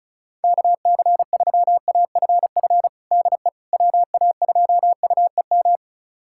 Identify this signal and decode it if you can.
Morse code, heard as a keyed tone, about 36 wpm, 700 Hz. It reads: KC3AFF DE WA2UEM
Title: USA Callsign DE Callsign - 6 Characters - 35wpm